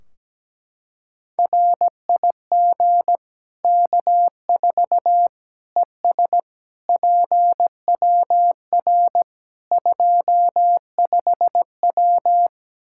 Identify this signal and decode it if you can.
Morse code, heard as a keyed tone, about 17 wpm, 700 Hz. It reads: RIG K4 ES PWR 25W
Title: Practice QSO - Exchange 2 - Info - 17wpm